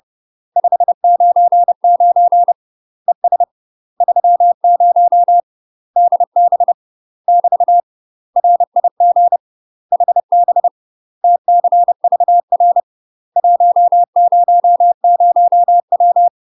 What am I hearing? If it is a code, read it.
Morse code, heard as a keyed tone, about 30 wpm, 700 Hz. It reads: 599 ES 30 DB <BT> RIG HB TCVR 100W
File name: UK MASTER QSOs TRAINING / 30WPM / UK MASTER QSOs 0000